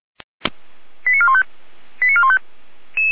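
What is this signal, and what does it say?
Five-tone selective call (ZVEI), heard as quick runs of five tones, tones of about 70 ms. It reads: 89415 89415
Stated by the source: ZVEI five-tone selective call received with an SDR.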